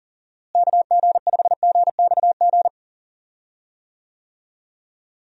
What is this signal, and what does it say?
Morse code, heard as a keyed tone, about 40 wpm, 700 Hz. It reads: KG5GXG
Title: USA Call Signs - 6 Characters - Repeated 2x - 40wpm